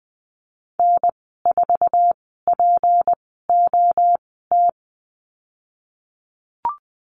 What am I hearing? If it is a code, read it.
Morse code, heard as a keyed tone, about 20 wpm, spaced as for 15 wpm, 700 Hz. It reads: N4POT
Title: 173 Lesson 28g - Number 9 - Call Signs 20wpm